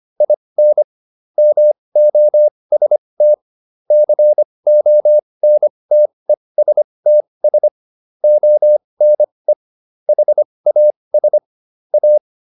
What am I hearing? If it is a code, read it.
Morse code, heard as a keyed tone, about 25 wpm, spaced as for 20 wpm, 600 Hz. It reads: IN MOST CONTESTS ONE HAS A